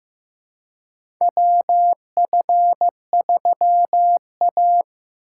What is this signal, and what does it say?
Morse code, heard as a keyed tone, about 15 wpm, 700 Hz. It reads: WF3A
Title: USA Call Signs - 4 Characters - 4x Character Spacing - 15wpm